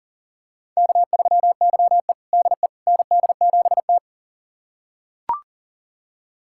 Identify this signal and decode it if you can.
Morse code, heard as a keyed tone, about 40 wpm, 700 Hz. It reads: K3YE DE ND7T
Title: USA Callsign DE Callsign - 4 Characters - 40wpm